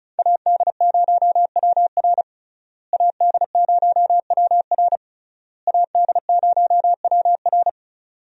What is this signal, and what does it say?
Morse code, heard as a keyed tone, about 34 wpm, 700 Hz. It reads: AD0WR AD0WR AD0WR
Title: USA Call Signs - 5 Characters - Repeated 3x - 35wpm